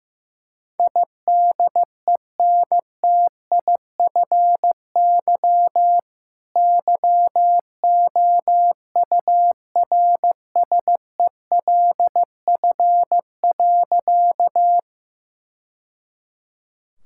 Morse code, heard as a keyed tone, about 16 wpm, 700 Hz. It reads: IDENTIFY YOURSELF.
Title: Sentences from Top 1000 Words - Review Part 6 - 15wpm